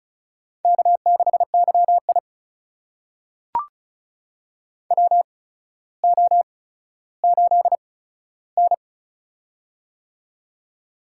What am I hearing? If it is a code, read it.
Morse code, heard as a keyed tone, about 36 wpm, spaced as for 15 wpm, 700 Hz. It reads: K6YI WO8N
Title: USA Call Signs - 4 Characters - 8x Character Spacing - 35wpm